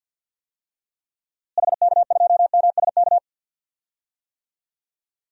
Morse code, heard as a keyed tone, about 47 wpm, 700 Hz. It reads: HK2MSK
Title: DXCC-100 Call Signs - 6 Characters - Repeated 3x - 50wpm